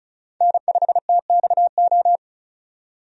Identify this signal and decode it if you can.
Morse code, heard as a keyed tone, about 36 wpm, 700 Hz. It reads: N5TXO
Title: USA Call Signs - 5 Characters - 35wpm